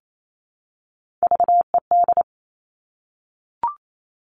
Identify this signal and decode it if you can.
Morse code, heard as a keyed tone, about 28 wpm, 700 Hz. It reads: VED
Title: All Cut Numbers - 3-Digit 28wpm